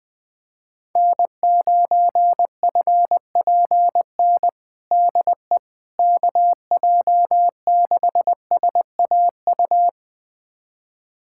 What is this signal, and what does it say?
Morse code, heard as a keyed tone, about 20 wpm, 700 Hz. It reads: N9FPN DE KJ6SAU
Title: USA Callsign DE Callsign 20wpm